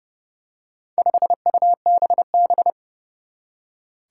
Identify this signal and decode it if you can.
Morse code, heard as a keyed tone, about 30 wpm, 700 Hz. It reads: 5UBB
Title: Four Challenging Letter-Number Combinations 30wpm